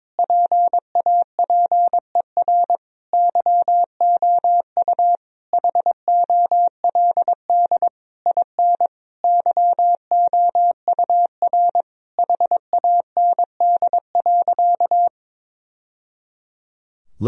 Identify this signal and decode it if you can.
Morse code, heard as a keyed tone, about 22 wpm, 700 Hz. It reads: PAPER YOU HOLD IN YOUR HAND.
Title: Sentences from Top 1000 Words - Review Part 2 - 22wpm